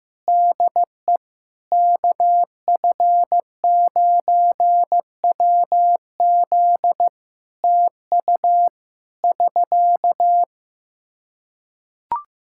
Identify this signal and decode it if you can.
Morse code, heard as a keyed tone, about 15 wpm, 700 Hz. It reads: DE KF9WZ TU <SK>
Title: Practice QSO - Exchange 4 - Ending - 15wpm